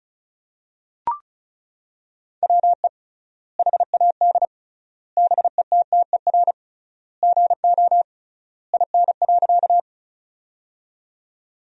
Morse code, heard as a keyed tone, about 36 wpm, 700 Hz. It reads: WE HAD BETTER GO IN.
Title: 3x Word-Spacing - Sentences from Top 400 Words - 35wpm